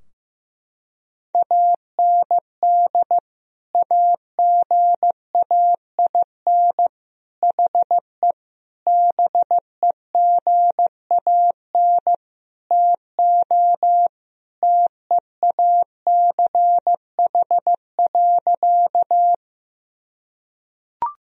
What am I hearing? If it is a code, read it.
Morse code, heard as a keyed tone, about 16 wpm, 700 Hz. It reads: AND AGAIN HE BEGAN TO TEACH.
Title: Sentences from Top 1000 Words - Review Part 2 - 15wpm